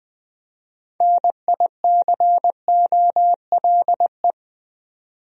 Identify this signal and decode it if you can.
Morse code, heard as a keyed tone, about 20 wpm, 700 Hz. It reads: NICOLE
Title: USA Common Names - 20wpm